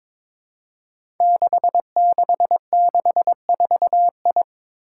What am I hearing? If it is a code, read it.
Morse code, heard as a keyed tone, about 22 wpm, 700 Hz. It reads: 6664I
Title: Five Challenging Letter-Number Combinations 22wpm